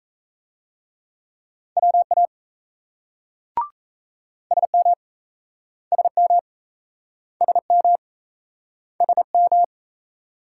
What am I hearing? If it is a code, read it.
Morse code, heard as a keyed tone, about 38 wpm, 700 Hz. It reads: WA SM SM SM SM